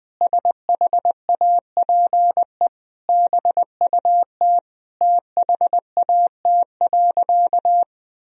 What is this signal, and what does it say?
Morse code, heard as a keyed tone, about 20 wpm, 700 Hz. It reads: SHAPE BUT THAT.
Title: Sentences from Top 1000 Words - Review Part 3 - 20wpm